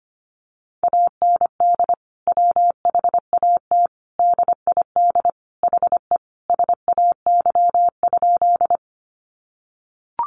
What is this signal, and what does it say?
Morse code, heard as a keyed tone, about 25 wpm, 700 Hz. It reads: AND WHAT DID HE SAY?
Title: Sentences from Top 200 Words 25wpm